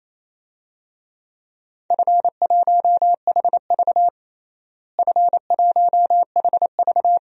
Morse code, heard as a keyed tone, about 28 wpm, 700 Hz. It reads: F1HV F1HV
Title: DXCC-100 Call Signs - 4 Characters - Repeated 3x - 28wpm